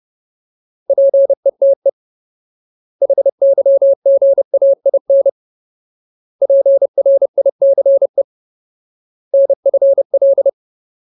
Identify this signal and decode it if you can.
Morse code, heard as a keyed tone, about 30 wpm, 550 Hz. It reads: PETE HYGAIN PRICE NFL
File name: Ultimate Word List / 30WPM Ultimate